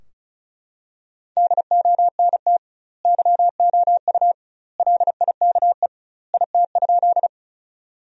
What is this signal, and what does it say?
Morse code, heard as a keyed tone, about 36 wpm, 700 Hz. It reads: DONT YOU LIKE IT?
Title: 2x Word-Spacing - Sentences from Top 300 Words - 35wpm